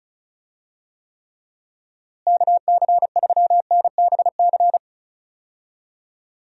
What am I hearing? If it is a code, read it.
Morse code, heard as a keyed tone, about 36 wpm, 700 Hz. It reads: KC3NBC
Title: USA Call Signs - 6 Characters - Repeated 2x - 35wpm